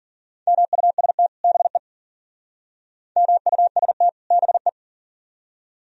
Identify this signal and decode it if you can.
Morse code, heard as a keyed tone, about 50 wpm, 700 Hz. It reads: MUST BE MUST BE
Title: Speed-Racing - Sets of 2 Words 40wpm